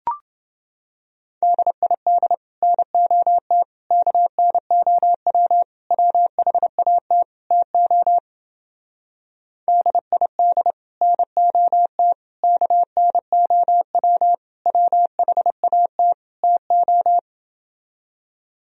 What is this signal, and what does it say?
Morse code, heard as a keyed tone, about 30 wpm, 700 Hz. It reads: DID NOT KNOW WHAT TO DID NOT KNOW WHAT TO